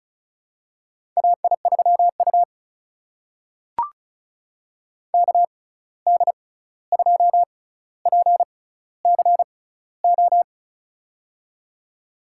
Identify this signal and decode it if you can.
Morse code, heard as a keyed tone, about 33 wpm, spaced as for 16 wpm, 700 Hz. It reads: AI3U KD2PCO
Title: USA Call Signs - 6x Character Spacing - 35wpm